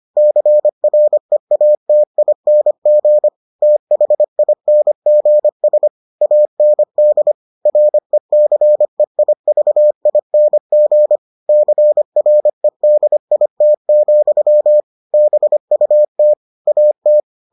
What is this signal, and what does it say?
Morse code, heard as a keyed tone, about 25 wpm, 600 Hz. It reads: CREATING THINGS AND RECEIVING CREDIT, BUT AT